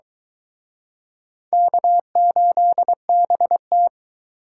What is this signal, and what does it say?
Morse code, heard as a keyed tone, about 23 wpm, 700 Hz. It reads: K8BT